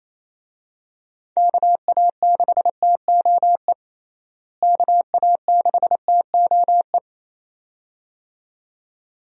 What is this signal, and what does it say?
Morse code, heard as a keyed tone, about 27 wpm, 700 Hz. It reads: KA6TOE KA6TOE
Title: USA Call Signs - 6 Characters - Repeated 2x - 28wpm